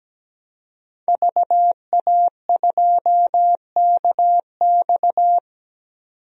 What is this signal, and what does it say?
Morse code, heard as a keyed tone, about 17 wpm, 700 Hz. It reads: VA2KX
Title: DXCC-100 Call Signs - 5 Characters - 17wpm